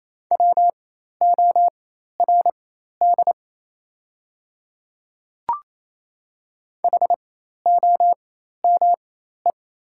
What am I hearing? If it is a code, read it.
Morse code, heard as a keyed tone, about 27 wpm, spaced as for 13 wpm, 690 Hz. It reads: WORD HOME
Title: 4x Character Spacing - Top 200 Words - 28wpm